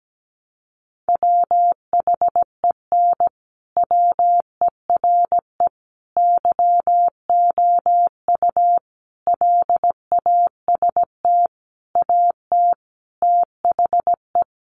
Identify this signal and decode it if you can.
Morse code, heard as a keyed tone, about 17 wpm, 700 Hz. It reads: WHEN WERE YOU LAST AT THE
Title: Sentences from Top 1300 Words 17wpm